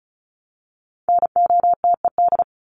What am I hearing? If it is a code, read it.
Morse code, heard as a keyed tone, about 34 wpm, 700 Hz. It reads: NOTED
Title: ED Suffix - Top 100 Words with Base Words 35wpm